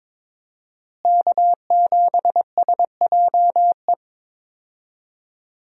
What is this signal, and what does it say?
Morse code, heard as a keyed tone, about 22 wpm, 700 Hz. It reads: K7SJE
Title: USA Call Signs - 5 Characters - Repeated 2x - 22wpm